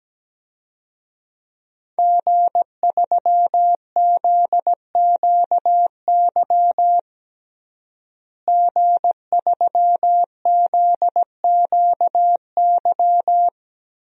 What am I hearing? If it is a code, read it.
Morse code, heard as a keyed tone, about 17 wpm, 700 Hz. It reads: G3ZQY G3ZQY
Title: DXCC-100 Call Signs - 5 Characters - Repeated 3x - 17wpm